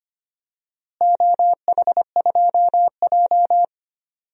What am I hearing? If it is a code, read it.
Morse code, heard as a keyed tone, about 25 wpm, 700 Hz. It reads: OH2J
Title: DXCC-100 Call Signs - 4 Characters - Repeated 2x - 25wpm